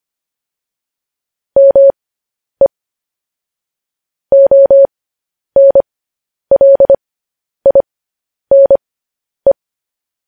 Morse code, heard as a keyed tone, about 25 wpm, spaced as for 10 wpm, 550 Hz. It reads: ME ONLINE
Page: news-headlines-in-morse-code-at-10-wpm